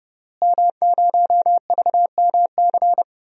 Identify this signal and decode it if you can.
Morse code, heard as a keyed tone, about 32 wpm, 700 Hz. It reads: M0VMC
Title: DXCC-100 Call Signs - 5 Characters - Repeated 2x - 30wpm